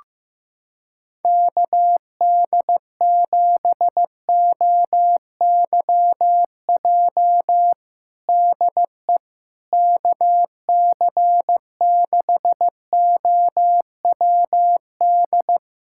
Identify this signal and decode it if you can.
Morse code, heard as a keyed tone, about 15 wpm, 700 Hz. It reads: KD7OYJ DE KC6OWD
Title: USA Callsign DE Callsign - 6 Characters - 15wpm